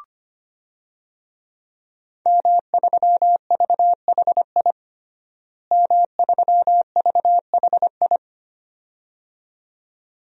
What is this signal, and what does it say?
Morse code, heard as a keyed tone, about 25 wpm, 700 Hz. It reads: M3VHI M3VHI
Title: DXCC-100 Call Signs - 5 Characters - Repeated 2x - 25wpm